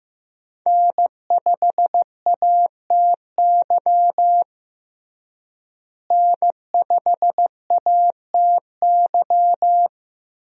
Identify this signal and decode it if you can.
Morse code, heard as a keyed tone, about 16 wpm, 700 Hz. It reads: N5ATY N5ATY
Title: USA Call Signs - 5 Characters - Repeated 2x - 15wpm